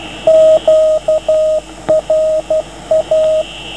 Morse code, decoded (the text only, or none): QRA